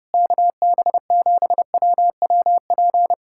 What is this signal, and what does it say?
Morse code, heard as a keyed tone, about 28 wpm, 700 Hz. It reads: KB7WWP
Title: Speed-Racing Common QSO Phrases - 30wpm